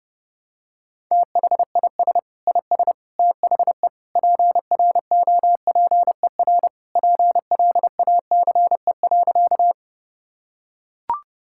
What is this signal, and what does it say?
Morse code, heard as a keyed tone, about 29 wpm, 700 Hz. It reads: THIS IS THE PROPER PLACE.
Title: Sentences from Top 1000 Words - Review Part 5 - 30wpm